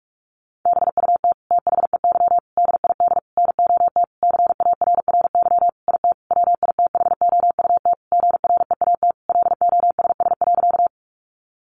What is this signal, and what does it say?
Morse code, heard as a keyed tone, about 44 wpm, 700 Hz. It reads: BUT THEY DID NOT CARRY IT WITHOUT GREAT LOSS.
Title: Sentences from Top 1400 Words 45wpm